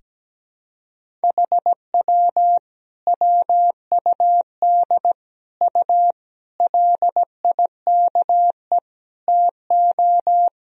Morse code, heard as a keyed tone, about 17 wpm, 700 Hz. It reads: HW WUD U LIKE TO
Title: Sentences from Top 100 Words and Abbreviations 17wpm